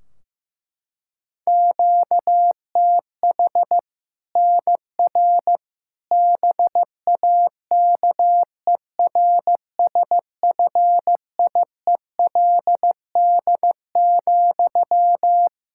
Morse code, heard as a keyed tone, about 15 wpm, 700 Hz. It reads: QTH NR BAKERSFIELD,
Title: Practice QSO - Exchange 1 - Beginning - 15wpm